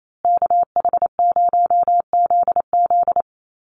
Morse code, heard as a keyed tone, about 28 wpm, 700 Hz. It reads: KH0ZZ
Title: Call Signs Repeated 2x - 28wpm